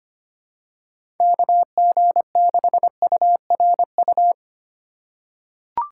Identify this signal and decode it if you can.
Morse code, heard as a keyed tone, about 25 wpm, 700 Hz. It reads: KG6URU
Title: USA Call Signs - 4x Character Spacing - 25wpm